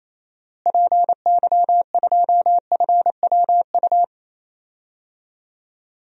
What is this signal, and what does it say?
Morse code, heard as a keyed tone, about 28 wpm, 700 Hz. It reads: PY2FWU
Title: DXCC-100 Call Signs - 6 Characters - Repeated 2x - 28wpm